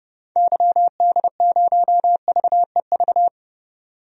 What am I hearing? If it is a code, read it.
Morse code, heard as a keyed tone, about 30 wpm, 700 Hz. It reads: YD0VEV
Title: DXCC-100 Call Signs - 6 Characters - 30wpm